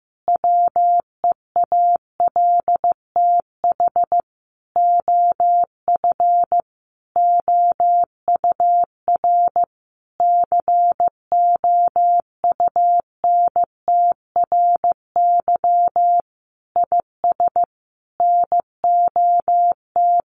Morse code, heard as a keyed tone, about 15 wpm, 700 Hz. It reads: WEALTH OF OUR COUNTRY IS NOT
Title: Sentences from Top 1200 Words 15wpm